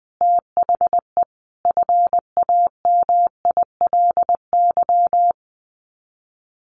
Morse code, heard as a keyed tone, about 20 wpm, 700 Hz. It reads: THE FAMILY
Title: Sets of 5 Words Encore2 20wpm